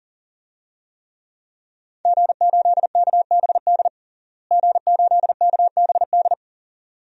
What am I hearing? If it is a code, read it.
Morse code, heard as a keyed tone, about 40 wpm, 700 Hz. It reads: G8KBD G8KBD
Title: DXCC-100 Call Signs - 5 Characters - Repeated 2x - 40wpm